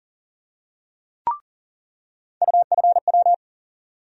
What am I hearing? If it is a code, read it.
Morse code, heard as a keyed tone, about 37 wpm, 700 Hz. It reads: UFW
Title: USA Call Sign Postfixes - 3 Letters 40wpm